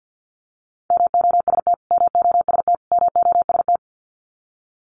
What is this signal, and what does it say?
Morse code, heard as a keyed tone, about 49 wpm, 700 Hz. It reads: MOST MOST MOST
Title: Top 200 Words Repeated 50wpm